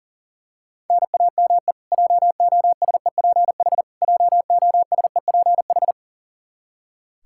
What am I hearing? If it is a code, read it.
Morse code, heard as a keyed tone, about 40 wpm, 700 Hz. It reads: NAME JOSEPH JOSEPH
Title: K1USN SST QSO Practice 40wpm